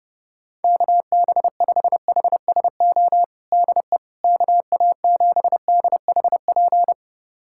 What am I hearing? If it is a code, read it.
Morse code, heard as a keyed tone, about 30 wpm, 700 Hz. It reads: KB5HSO DE KA7DHP
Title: USA Callsign DE Callsign 30wpm